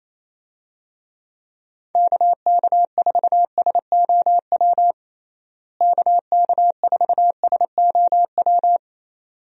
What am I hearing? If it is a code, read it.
Morse code, heard as a keyed tone, about 27 wpm, 700 Hz. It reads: KK4SOW KK4SOW
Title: USA Call Signs - 6 Characters - Repeated 2x - 28wpm